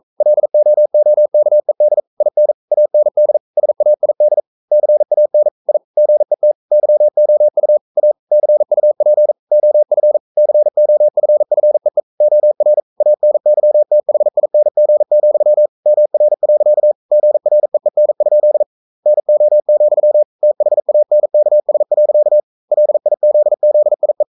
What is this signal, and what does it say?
Morse code, heard as a keyed tone, about 42 wpm, 600 Hz. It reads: LOOKED IN AND SAID CAN I GET YOU A CUP OF COFFEE OR ANYTHING, MR. GREEN? NO, THANKS. LIZZIE
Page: Romanen Never af Ken Follet på engelsk.